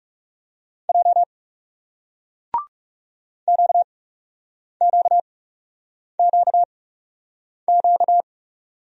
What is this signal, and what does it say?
Morse code, heard as a keyed tone, about 39 wpm, spaced as for 8 wpm, 700 Hz. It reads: J QQQQ